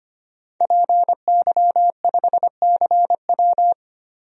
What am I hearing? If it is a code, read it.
Morse code, heard as a keyed tone, about 25 wpm, 690 Hz. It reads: PY5CW
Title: DXCC-100 Call Signs - 5 Characters - 25wpm